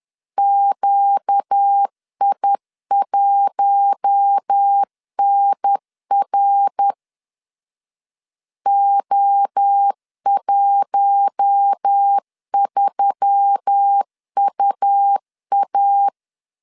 Morse code, heard as a keyed tone, about 10 wpm, 800 Hz. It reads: QI1NR O13UA
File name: LEZIONI / 6 PKX0 / 6 PKX 10wpm 2426